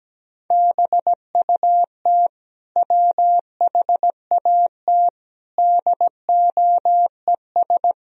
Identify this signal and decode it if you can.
Morse code, heard as a keyed tone, about 17 wpm, 700 Hz. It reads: BUT WHAT DOES
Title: Sentences from Top 1000 Words - Review Part 2 - 17wpm